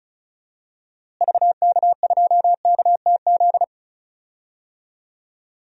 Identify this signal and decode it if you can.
Morse code, heard as a keyed tone, about 34 wpm, 700 Hz. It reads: VK2KTZ
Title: DXCC-100 Call Signs - 6 Characters - Repeated 2x - 35wpm